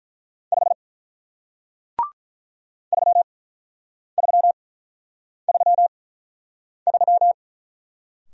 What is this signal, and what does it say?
Morse code, heard as a keyed tone, about 43 wpm, 700 Hz. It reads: 5 3 3 3 3
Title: Speed-Racing Single Letter-Number 35wpm